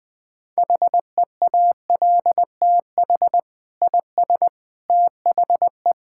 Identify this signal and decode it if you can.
Morse code, heard as a keyed tone, about 20 wpm, 700 Hz. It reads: HEALTH IS THE
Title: Sentences from Top 1000 Words - Review Part 5 - 20wpm